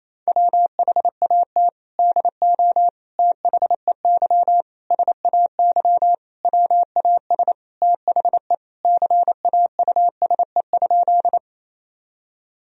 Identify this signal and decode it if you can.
Morse code, heard as a keyed tone, about 28 wpm, 700 Hz. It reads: WHAT DO THEY SAY WAS THE CAUSE?
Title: Sentences from Top 1000 Words - Review Part 1 - 28wpm